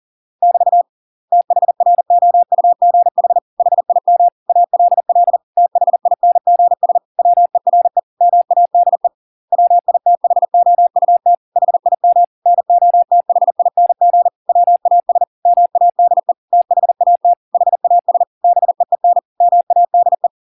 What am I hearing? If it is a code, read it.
Morse code, heard as a keyed tone, about 40 wpm, 700 Hz. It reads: <BT> THROUGH HIM ALL THINGS WERE MADE WITHOUT HIM NOTHING WAS MADE THAT HAS BEEN MADE